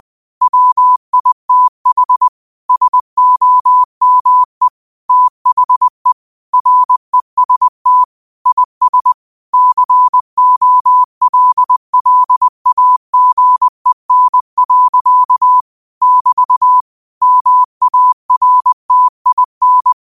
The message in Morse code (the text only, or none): WITH SOME THE REST IS COLLAGEN. <BT> MARTIN